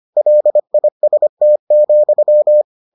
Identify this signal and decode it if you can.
Morse code, heard as a keyed tone, about 25 wpm, 600 Hz. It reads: LIST,